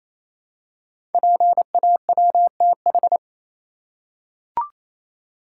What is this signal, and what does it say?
Morse code, heard as a keyed tone, about 27 wpm, 700 Hz. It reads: PAWTH
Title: Code Groups - 5 Letters - 28wpm